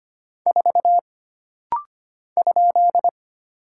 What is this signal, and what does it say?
Morse code, heard as a keyed tone, about 25 wpm, 700 Hz. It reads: 4 ?
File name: Single Character (Letter:Number:Punctuation:Prosign) - Rapid-Fire - 25wpm